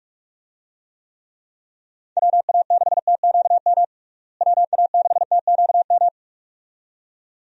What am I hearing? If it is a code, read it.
Morse code, heard as a keyed tone, about 42 wpm, 700 Hz. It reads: WA6TQM WA6TQM
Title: USA Call Signs - 6 Characters - Repeated 2x - 45wpm